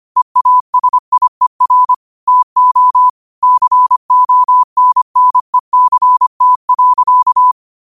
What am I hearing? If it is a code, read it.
Morse code, heard as a keyed tone, about 25 wpm, 1000 Hz. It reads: EASIER TO CONNECT.